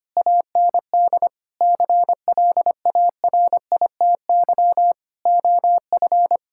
Morse code, heard as a keyed tone, about 25 wpm, 700 Hz. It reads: AND CLARITY OF
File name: practice / 25wpm / Difficult Sentences with Hints - 25wpm